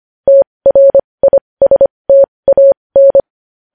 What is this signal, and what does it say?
Morse code, heard as a keyed tone, about 25 wpm, spaced as for 20 wpm, 550 Hz. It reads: TRISTAN
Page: news-headlines-in-morse-code-at-20-wpm